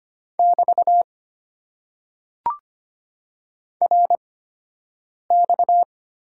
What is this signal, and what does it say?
Morse code, heard as a keyed tone, about 25 wpm, spaced as for 7 wpm, 700 Hz. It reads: <BT> RX
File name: Top 90 QSO Elements - 8x Character Spacing - 25wpm